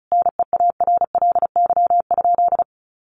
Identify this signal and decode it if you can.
Morse code, heard as a keyed tone, about 34 wpm, 700 Hz. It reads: NEARLY?
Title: Sentences from Top 1300 Words 35wpm